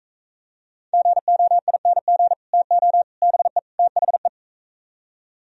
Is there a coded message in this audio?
Yes: Morse code from 0.9 to 4.3 s, heard as a keyed tone, about 45 wpm, 700 Hz. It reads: GOING TO BE THE